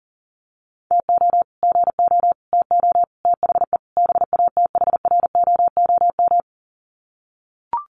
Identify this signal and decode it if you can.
Morse code, heard as a keyed tone, about 40 wpm, 700 Hz. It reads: TO GO TO THE BATHROOM
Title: Sets of 5 Words Encore 40wpm